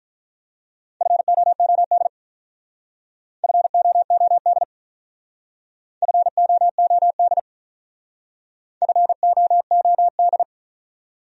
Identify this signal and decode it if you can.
Morse code, heard as a keyed tone, about 45 wpm, 700 Hz. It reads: FOOD FOOD FOOD FOOD